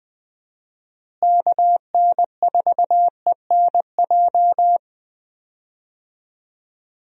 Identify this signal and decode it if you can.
Morse code, heard as a keyed tone, about 20 wpm, 700 Hz. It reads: KN4ENJ
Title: USA Call Signs - 6 Characters - Repeated 2x - 20wpm